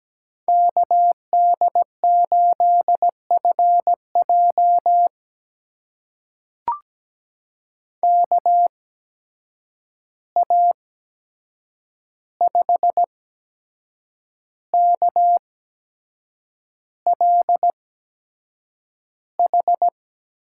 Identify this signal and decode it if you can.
Morse code, heard as a keyed tone, about 17 wpm, spaced as for 7 wpm, 700 Hz. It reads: KD8FJ KA5KLH
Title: USA Call Signs - 8x Character Spacing - 17wpm